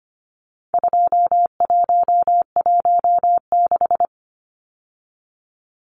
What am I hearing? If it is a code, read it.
Morse code, heard as a keyed tone, about 25 wpm, 700 Hz. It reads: 2116
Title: Numbers - 4-Digit 25wpm